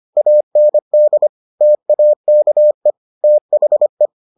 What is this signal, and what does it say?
Morse code, heard as a keyed tone, about 25 wpm, 600 Hz. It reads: AND TAKE THE